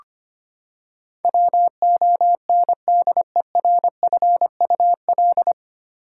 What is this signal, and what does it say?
Morse code, heard as a keyed tone, about 25 wpm, 700 Hz. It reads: WONDERFUL
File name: Nine Letter Words - 25wpm